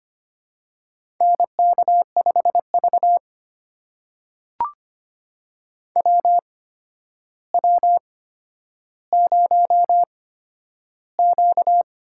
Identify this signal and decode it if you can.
Morse code, heard as a keyed tone, about 25 wpm, spaced as for 11 wpm, 700 Hz. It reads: NK5V WW0Q
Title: USA Call Signs - 8x Character Spacing - 25wpm